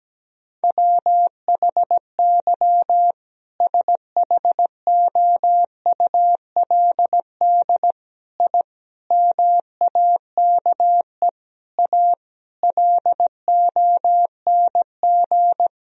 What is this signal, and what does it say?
Morse code, heard as a keyed tone, about 17 wpm, 700 Hz. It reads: WHY SHOULD I MAKE A LONG
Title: Sentences from Top 1000 Words - Review Part 1 - 17wpm